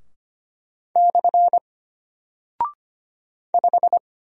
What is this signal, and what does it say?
Morse code, heard as a keyed tone, about 25 wpm, 700 Hz. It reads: / 5
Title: Single Character (Letter:Number:Punctuation:Prosign) Alt 25wpm